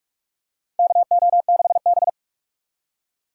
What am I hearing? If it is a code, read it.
Morse code, heard as a keyed tone, about 46 wpm, 700 Hz. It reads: KO6B
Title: USA Call Signs - 4 Characters - 4x Character Spacing - 45wpm